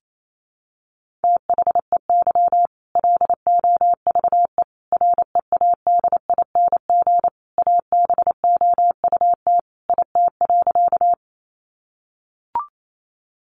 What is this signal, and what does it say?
Morse code, heard as a keyed tone, about 28 wpm, 700 Hz. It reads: THEY LOVE READING ABOUT IT.
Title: Sentences from Top 600 Words 28wpm